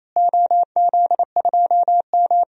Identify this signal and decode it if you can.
Morse code, heard as a keyed tone, about 28 wpm, 700 Hz. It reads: OZ2M
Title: DXCC-100 Call Signs - 4 Characters - Repeated 2x - 28wpm